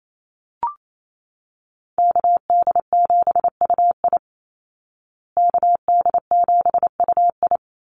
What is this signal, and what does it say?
Morse code, heard as a keyed tone, about 28 wpm, 700 Hz. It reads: KD7UI KD7UI
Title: Call Signs Repeated 2x - 28wpm